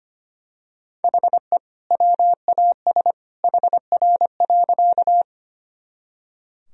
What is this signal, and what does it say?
Morse code, heard as a keyed tone, about 25 wpm, 700 Hz. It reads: HE WAS HR.
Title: Sentences from Top 200 Words and Abbreviations 25wpm